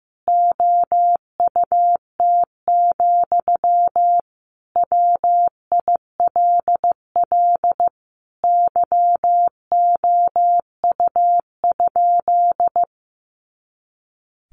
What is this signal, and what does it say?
Morse code, heard as a keyed tone, about 15 wpm, 700 Hz. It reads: OUT, WILL YOU?
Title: Sentences from Top 300 Words 15wpm